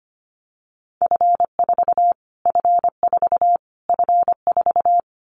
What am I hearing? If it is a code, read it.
Morse code, heard as a keyed tone, about 25 wpm, 700 Hz. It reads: F4 F4 F4
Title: Callsign Prefixes for Top 25 DXCC - Repeated 3x - 25wpm